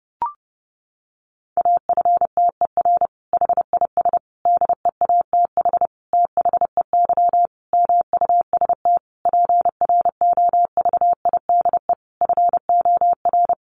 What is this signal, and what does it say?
Morse code, heard as a keyed tone, about 30 wpm, 700 Hz. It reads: AFTER HIS DEATH THEY MUST PROVIDE FOR
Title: Sentences from Top 1100 Words 30wpm